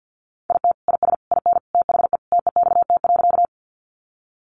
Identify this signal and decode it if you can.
Morse code, heard as a keyed tone, about 50 wpm, 700 Hz. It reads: IT IS IN THE TEXT.
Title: Sentences from Top 1200 Words 50wpm